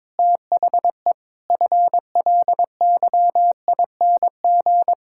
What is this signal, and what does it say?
Morse code, heard as a keyed tone, about 22 wpm, 700 Hz. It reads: THE FLYING
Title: Pangram Challenge v2 - 22wpm